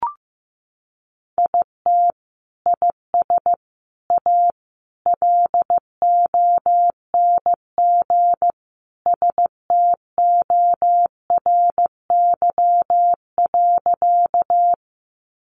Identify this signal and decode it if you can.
Morse code, heard as a keyed tone, about 15 wpm, 700 Hz. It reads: IT IS A LONG STORY.
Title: Sentences from Top 300 Words 15wpm